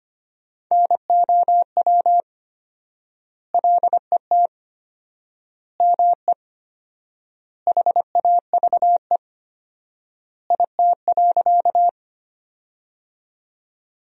Morse code, heard as a keyed tone, about 25 wpm, 700 Hz. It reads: NOW LET ME HAVE IT.
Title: 4x Word-Spacing - Sentences from Top 300 Words - 25wpm